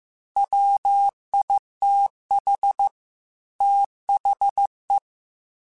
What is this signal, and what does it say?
Morse code, heard as a keyed tone, about 15 wpm, 800 Hz. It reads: WITH THE